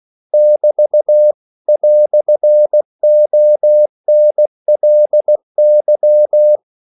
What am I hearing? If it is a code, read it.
Morse code, heard as a keyed tone, about 16 wpm, 600 Hz. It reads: <BT> "ONLY
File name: CW / Practice / EnglishQuotations0116